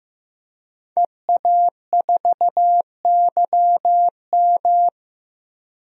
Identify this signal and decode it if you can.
Morse code, heard as a keyed tone, about 15 wpm, 700 Hz. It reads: EA4YM
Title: DXCC-100 Call Signs - 5 Characters - 15wpm